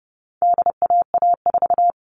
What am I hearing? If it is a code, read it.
Morse code, heard as a keyed tone, about 30 wpm, 700 Hz. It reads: DAA4
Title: All Cut Numbers - 4-Digit 30wpm